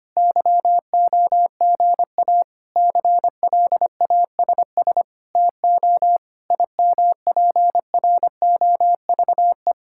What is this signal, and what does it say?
Morse code, heard as a keyed tone, about 25 wpm, 700 Hz. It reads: YOGA CLASS TO IMPROVE
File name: practice / 25wpm / Difficult Sentences with Hints - 25wpm